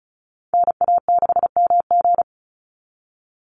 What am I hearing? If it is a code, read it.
Morse code, heard as a keyed tone, about 34 wpm, 700 Hz. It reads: NA6MG
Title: NA SOTA Chaser Callsigns 35wpm